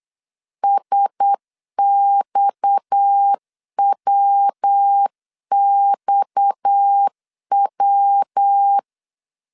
Morse code, heard as a keyed tone, about 8 wpm, 800 Hz. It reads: SXWXW
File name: 8wpm